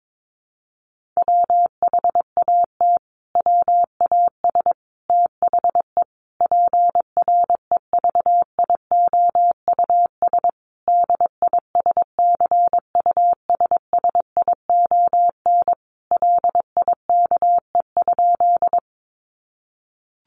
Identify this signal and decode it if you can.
Morse code, heard as a keyed tone, about 22 wpm, 700 Hz. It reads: WHAT WAS THE PREVIOUS DISCUSSION LIKE?
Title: Sentences from Top 1500 Words 22wpm